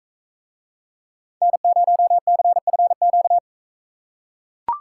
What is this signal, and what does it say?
Morse code, heard as a keyed tone, about 42 wpm, 700 Hz. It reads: N0CFQ